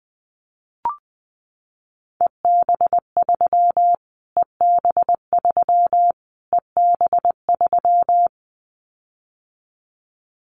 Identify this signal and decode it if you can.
Morse code, heard as a keyed tone, about 19 wpm, 700 Hz. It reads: EB3 EB3 EB3
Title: Callsign Prefixes for Top 25 DXCC - Repeated 3x - 20wpm